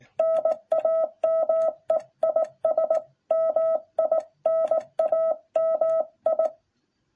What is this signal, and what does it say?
Morse code, heard as a keyed tone, about 17 wpm, 660 Hz. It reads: NAME IS MINAMI